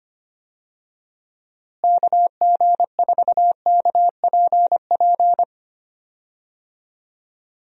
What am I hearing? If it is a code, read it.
Morse code, heard as a keyed tone, about 25 wpm, 700 Hz. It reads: KG4KPP
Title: USA Call Signs - 6 Characters - Repeated 3x - 25wpm